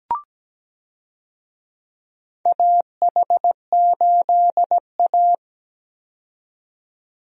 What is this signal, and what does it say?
Morse code, heard as a keyed tone, about 17 wpm, 700 Hz. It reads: AH8A
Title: USA Call Signs - 4 Characters - 17wpm